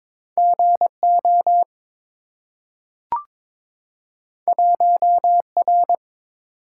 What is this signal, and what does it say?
Morse code, heard as a keyed tone, about 23 wpm, 700 Hz. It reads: GO 1R
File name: practice / 22wpm / Two Letter-Number Combinations - 22wpm